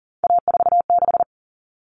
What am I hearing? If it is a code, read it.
Morse code, heard as a keyed tone, about 40 wpm, 700 Hz. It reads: A46
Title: Common Cut Numbers - 3-Digit 40wpm